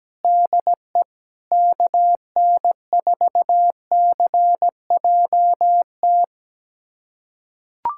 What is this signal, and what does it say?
Morse code, heard as a keyed tone, about 17 wpm, 700 Hz. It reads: DE KN4CJT
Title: USA Callsign DE Callsign - 6 Characters - 17wpm